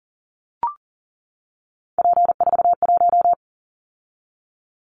Morse code, heard as a keyed tone, about 42 wpm, 700 Hz. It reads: P41
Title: Callsign Prefixes for Top 100 DXCC - 40wpm